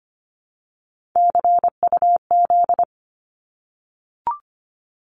Morse code, heard as a keyed tone, about 25 wpm, 700 Hz. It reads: CUZ
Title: Abbreviations 25wpm